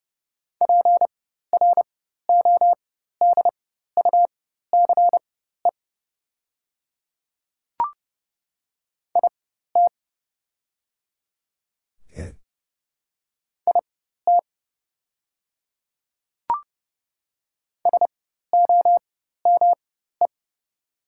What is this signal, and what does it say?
Morse code, heard as a keyed tone, about 30 wpm, spaced as for 14 wpm, 690 Hz. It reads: PRODUCE IT IT SOME
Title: 4x Character Spacing - Top 500 Words - 30wpm